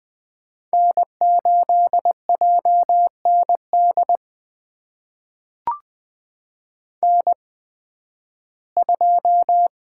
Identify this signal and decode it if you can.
Morse code, heard as a keyed tone, about 20 wpm, spaced as for 13 wpm, 700 Hz. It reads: N8JND N2
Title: USA Call Signs - 5 Characters - 8x Character Spacing - 20wpm